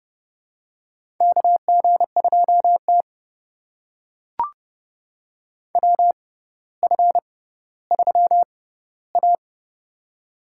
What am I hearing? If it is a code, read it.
Morse code, heard as a keyed tone, about 30 wpm, spaced as for 15 wpm, 700 Hz. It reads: KG2T WF3A
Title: USA Call Signs - 4 Characters - 6x Character Spacing - 30wpm